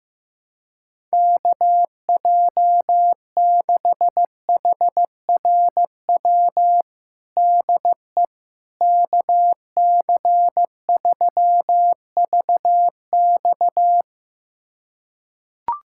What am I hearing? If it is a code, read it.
Morse code, heard as a keyed tone, about 15 wpm, 700 Hz. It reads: KJ6HRW DE KC3VX
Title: USA Callsign DE Callsign 15wpm